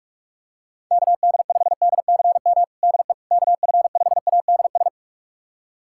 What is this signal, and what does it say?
Morse code, heard as a keyed tone, about 46 wpm, 700 Hz. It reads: KD5DCM DE KF5ADS
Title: USA Callsign DE Callsign 45wpm